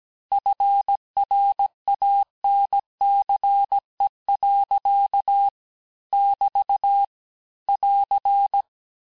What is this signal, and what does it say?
Morse code, heard as a keyed tone, about 17 wpm, 800 Hz. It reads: FRANCE. <BT> <AR>
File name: cours-du-lundi files / cours-1587-texte-clair-vitesse-16mpm-puis-21mpm